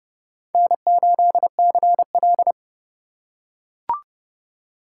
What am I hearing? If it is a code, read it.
Morse code, heard as a keyed tone, about 28 wpm, 700 Hz. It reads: N8CL